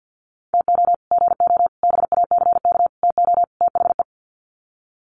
Morse code, heard as a keyed tone, about 50 wpm, 700 Hz. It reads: TO GO BACK TO THE
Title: Sets of 5 Words 50wpm